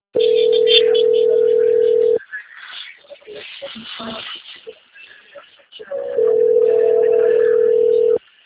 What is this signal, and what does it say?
Ringback tone (North America), 440 + 480 Hz, with speech.
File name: out- - - - - .20491